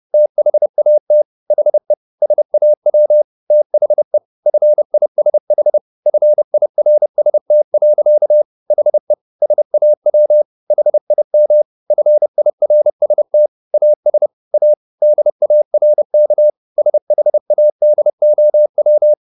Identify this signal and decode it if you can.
Morse code, heard as a keyed tone, about 30 wpm, 600 Hz. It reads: THAT HE SAW THE FISH FIRST. HE SAW HIM FIRST AS A DARK SHADOW